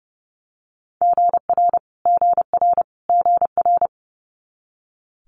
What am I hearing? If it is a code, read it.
Morse code, heard as a keyed tone, about 30 wpm, 700 Hz. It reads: GR GR GR
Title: Multi-letter Phonemes Repeated 3x 30wpm